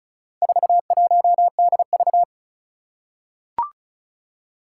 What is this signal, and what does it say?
Morse code, heard as a keyed tone, about 36 wpm, 700 Hz. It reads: 41DV
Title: Four Challenging Letter-Number Combinations 35wpm